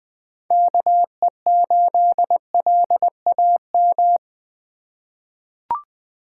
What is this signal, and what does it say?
Morse code, heard as a keyed tone, about 20 wpm, 700 Hz. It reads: KE8LAM
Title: USA Call Signs - 6 Characters - 3x Character Spacing - 20wpm